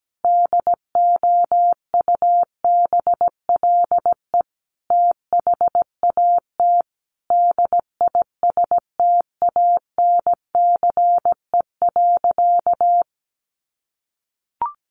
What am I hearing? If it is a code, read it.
Morse code, heard as a keyed tone, about 17 wpm, 700 Hz. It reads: DOUBLE THAT DISTANCE.
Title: Sentences from Top 1400 Words 17wpm